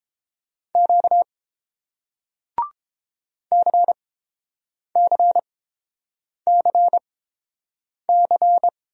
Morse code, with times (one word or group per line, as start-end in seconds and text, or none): 0.8-1.2 Q
3.5-8.7 CCCC